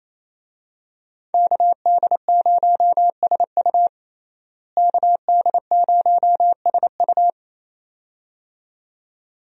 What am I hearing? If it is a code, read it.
Morse code, heard as a keyed tone, about 28 wpm, 700 Hz. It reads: KD0SU KD0SU
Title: USA Call Signs - 5 Characters - Repeated 2x - 28wpm